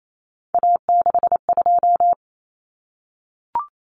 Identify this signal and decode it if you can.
Morse code, heard as a keyed tone, about 28 wpm, 700 Hz. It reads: A62
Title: Common Cut Numbers - 3-Digit 28wpm